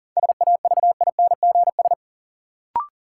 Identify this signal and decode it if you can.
Morse code, heard as a keyed tone, about 40 wpm, 700 Hz. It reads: SAVINGS
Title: Sentences from Top 1200 Words - New Word Drill Alt - 40wpm